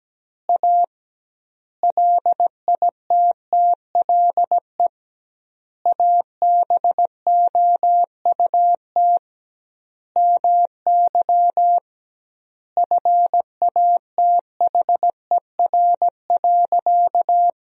Morse code, heard as a keyed tone, about 17 wpm, 700 Hz. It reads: A LITTLE ABOUT MY FATHER.
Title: 2x Word-Spacing - Sentences from Top 400 Words - 17wpm